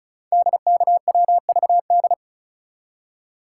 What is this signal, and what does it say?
Morse code, heard as a keyed tone, about 36 wpm, 690 Hz. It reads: DKWVD
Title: Code Groups - 5 Letters - 35wpm